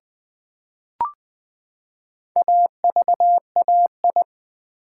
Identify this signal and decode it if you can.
Morse code, heard as a keyed tone, about 20 wpm, 700 Hz. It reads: AVAI
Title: Build A Word - 4 Letters At A Time - 12-Letter Words - 20wpm